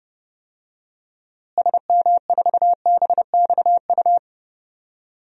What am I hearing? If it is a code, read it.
Morse code, heard as a keyed tone, about 30 wpm, 700 Hz. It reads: SM4BXU
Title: DXCC-100 Call Signs - 6 Characters - Repeated 3x - 30wpm